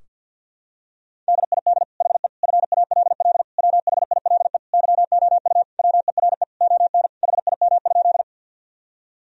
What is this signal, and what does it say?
Morse code, heard as a keyed tone, about 48 wpm, 700 Hz. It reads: DID HE FALL WHILE YOU WERE ON HIM?
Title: Sentences from Top 1000 Words - Review Part 2 - 50wpm